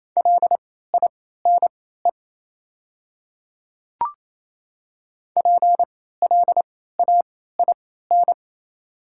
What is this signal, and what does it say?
Morse code, heard as a keyed tone, about 28 wpm, spaced as for 16 wpm, 700 Hz. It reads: LINE PLAIN
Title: 3x Character Spacing - Top 500 Words - 28wpm